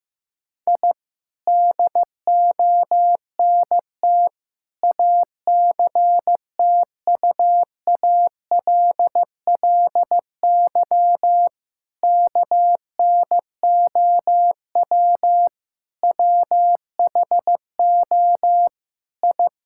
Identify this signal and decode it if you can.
Morse code, heard as a keyed tone, about 16 wpm, 700 Hz. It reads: I DONT ACTUALLY KNOW WHO I
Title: Sentences from Top 1000 Words - Review Part 5 - 15wpm